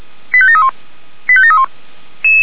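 Five-tone selective call (ZVEI), heard as quick runs of five tones, tones of about 70 ms. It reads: 85731 85731
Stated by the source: ZVEI five-tone call 85385 received by SDR.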